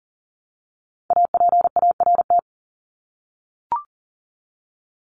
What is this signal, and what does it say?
Morse code, heard as a keyed tone, about 40 wpm, 700 Hz. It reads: APART
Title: Top 1501-1750 Words 40wpm